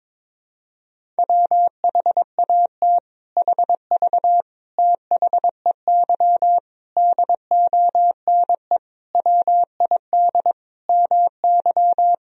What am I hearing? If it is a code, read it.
Morse code, heard as a keyed tone, about 21 wpm, 700 Hz. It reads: WHAT HV THEY DONE WID MY
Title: Sentences from Top 500 Words and Abbreviations 22wpm